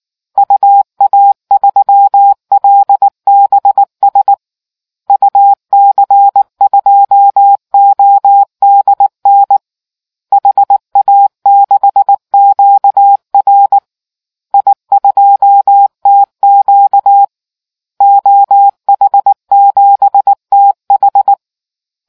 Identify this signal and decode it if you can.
Morse code, heard as a keyed tone, about 19 wpm, 800 Hz. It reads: UA3LBS UC2ODN HA6QR I2TQ OH7TH